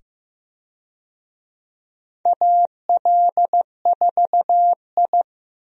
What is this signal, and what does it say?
Morse code, heard as a keyed tone, about 15 wpm, 700 Hz. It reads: AL4I
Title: USA Call Signs - 4 Characters - 15wpm